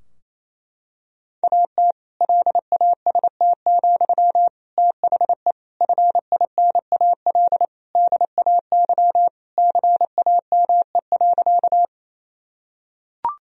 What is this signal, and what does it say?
Morse code, heard as a keyed tone, about 28 wpm, 700 Hz. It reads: AT LAST, THE FINAL DAY CAME.
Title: Sentences from Top 1000 Words - Review Part 2 - 28wpm